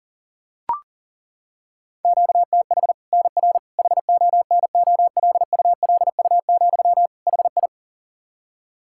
Morse code, heard as a keyed tone, about 40 wpm, 700 Hz. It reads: QTH NR HONOLULU, HI
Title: Practice QSO - Exchange 1 - Beginning - 40wpm